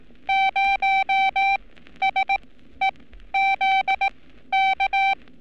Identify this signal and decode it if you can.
Morse code, heard as a keyed tone, about 18 wpm, spaced as for 12 wpm, 780 Hz. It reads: 0SEZK
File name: training / NAVPERS 11224-11235 / NAVPERS 11227 RA RB